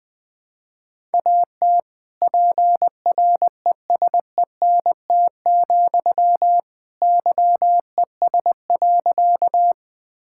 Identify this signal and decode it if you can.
Morse code, heard as a keyed tone, about 20 wpm, 700 Hz. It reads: AT PRESENT, YES.
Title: Sentences from Top 1000 Words - Review Part 5 - 20wpm